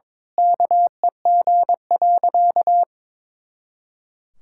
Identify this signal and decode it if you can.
Morse code, heard as a keyed tone, about 22 wpm, 700 Hz. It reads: KEG.
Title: Pangram Challenge v2 - 22wpm